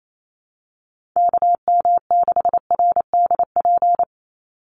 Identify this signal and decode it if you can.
Morse code, heard as a keyed tone, about 28 wpm, 700 Hz. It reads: KM6RDP
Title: USA Call Signs 28wpm